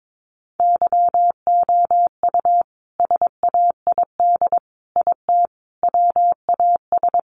Morse code, heard as a keyed tone, about 22 wpm, 700 Hz. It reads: YOU SAID IT WAS
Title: Sentences from Top 1100 Words 22wpm